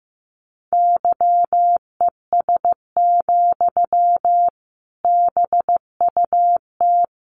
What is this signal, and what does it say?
Morse code, heard as a keyed tone, about 15 wpm, 700 Hz. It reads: YES, BUT
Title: Sentences from Top 1200 Words 15wpm